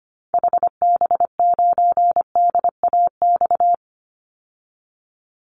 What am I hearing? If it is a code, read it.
Morse code, heard as a keyed tone, about 25 wpm, 700 Hz. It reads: HB9DAX
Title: Call Signs Repeated 2x - 25wpm